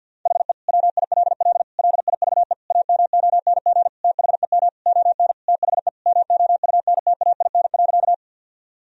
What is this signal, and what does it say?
Morse code, heard as a keyed tone, about 50 wpm, 700 Hz. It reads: HE WILL LIVE AMONG THEM ON THE MOUNTAIN.
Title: Sentences from Top 1000 Words - Review Part 3 - 50wpm